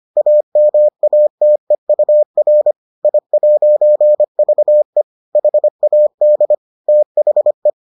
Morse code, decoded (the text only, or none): AMATEUR I'VE HAD THE